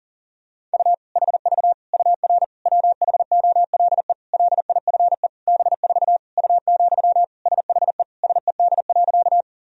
Morse code, heard as a keyed tone, about 37 wpm, 700 Hz. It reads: U HV UR WHOLE LIFE B4 U, SHE SED.
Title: Sentences from Top 400 Words and Abbreviations 40wpm